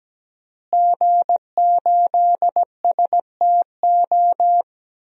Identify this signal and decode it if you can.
Morse code, heard as a keyed tone, about 17 wpm, 700 Hz. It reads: G8STO
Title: DXCC-100 Call Signs - 5 Characters - Repeated 3x - 17wpm